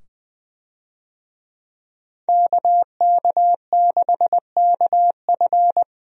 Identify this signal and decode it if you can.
Morse code, heard as a keyed tone, about 20 wpm, 700 Hz. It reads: KK6KF
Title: USA Call Signs - 5 Characters - 20wpm